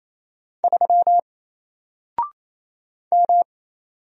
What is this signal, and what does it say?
Morse code, heard as a keyed tone, about 27 wpm, 700 Hz. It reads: 3 M
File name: Single-Letter-Number-28wpm